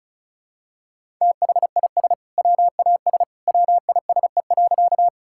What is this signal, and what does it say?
Morse code, heard as a keyed tone, about 34 wpm, 700 Hz. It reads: THIS WAS WISE.
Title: Sentences from Top 1600 Words 35wpm